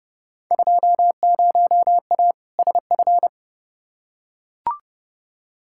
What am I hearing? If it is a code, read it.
Morse code, heard as a keyed tone, about 31 wpm, 700 Hz. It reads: 20A SF
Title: Field Day Exchanges Encore - 30wpm